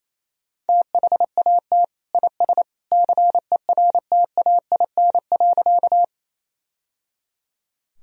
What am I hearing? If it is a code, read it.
Morse code, heard as a keyed tone, about 28 wpm, 700 Hz. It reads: THAT IS CERTAIN.
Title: Sentences from Top 1000 Words - Review Part 2 - 28wpm